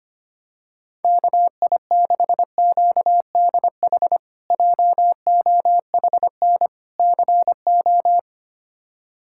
Morse code, heard as a keyed tone, about 25 wpm, 700 Hz. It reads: KI6QDH JOHN CO
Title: K1USN SST QSO Practice - 25wpm